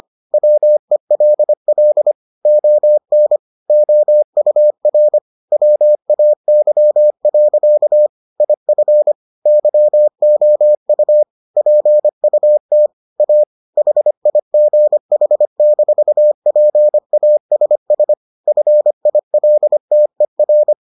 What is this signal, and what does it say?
Morse code, heard as a keyed tone, about 25 wpm, 600 Hz. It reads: WELL ON OUR WAY. IF YOU PUT A HIGH-PASS FILTER